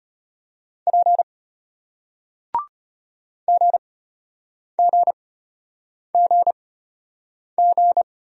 Morse code, heard as a keyed tone, about 33 wpm, spaced as for 8 wpm, 690 Hz. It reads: P GGGG